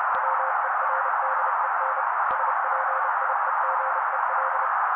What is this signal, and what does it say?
Morse code, heard as a keyed tone, about 30 wpm, 530 Hz. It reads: WERDEN SPIEGEL
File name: daily / 29wpm noise